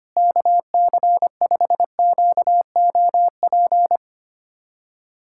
Morse code, heard as a keyed tone, about 25 wpm, 700 Hz. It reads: KC5QOP
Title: USA Callsign DE Callsign 25wpm